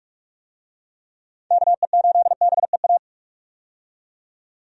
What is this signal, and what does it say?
Morse code, heard as a keyed tone, about 47 wpm, 700 Hz. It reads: KE8BEA
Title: USA Call Signs - 6 Characters - 45wpm